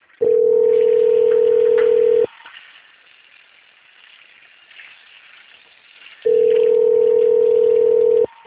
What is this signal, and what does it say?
Ringback tone (North America), 440 + 480 Hz.
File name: out- - - - - .14463